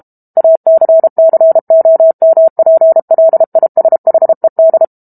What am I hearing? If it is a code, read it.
Morse code, heard as a keyed tone, about 32 wpm, 650 Hz. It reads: ACCOMPLISHED
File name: Book Two / 06-Chapter 6 - 32 WPM